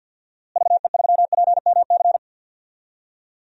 Morse code, heard as a keyed tone, about 50 wpm, 700 Hz. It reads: VE3PMC
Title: DXCC-100 Call Signs - 6 Characters - Repeated 3x - 50wpm